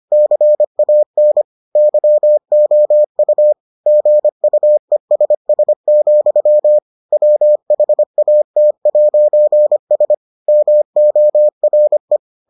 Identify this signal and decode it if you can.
Morse code, heard as a keyed tone, about 25 wpm, 600 Hz. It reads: CAN YOU GUESS, WHAT'S MORE